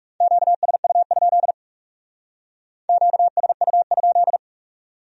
Morse code, heard as a keyed tone, about 37 wpm, 700 Hz. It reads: QSU? QSU?
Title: Speed-Racing Q-Codes - 30wpm